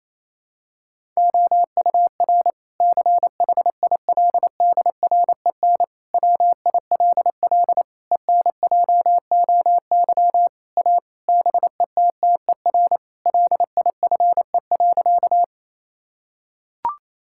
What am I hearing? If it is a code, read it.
Morse code, heard as a keyed tone, about 28 wpm, 700 Hz. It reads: OUR CHILDREN WILL ENJOY A BETTER LIFE.
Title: Sentences from Top 1000 Words - Review Part 5 - 28wpm